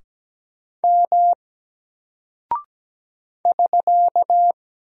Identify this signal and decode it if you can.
Morse code, heard as a keyed tone, about 17 wpm, 700 Hz. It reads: M <SK>
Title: Single Character (Letter:Number:Punctuation:Prosign) Alt 17wpm